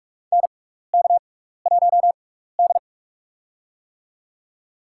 Morse code, heard as a keyed tone, about 42 wpm, spaced as for 15 wpm, 700 Hz. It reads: NK1D
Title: USA Call Signs - 4 Characters - 6x Character Spacing - 45wpm